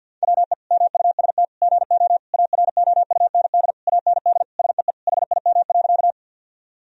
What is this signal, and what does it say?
Morse code, heard as a keyed tone, about 50 wpm, 700 Hz. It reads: WE MUST GO AROUND AND SEE HIM.
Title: Sentences from Top 1000 Words - Review Part 3 - 50wpm